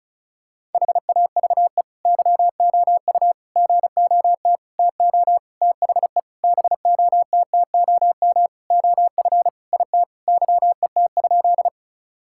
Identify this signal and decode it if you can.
Morse code, heard as a keyed tone, about 36 wpm, 700 Hz. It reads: HAVE YOU GOT TO THE BOTTOM OF IT YET?
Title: Sentences from Top 1000 Words - Review Part 6 - 35wpm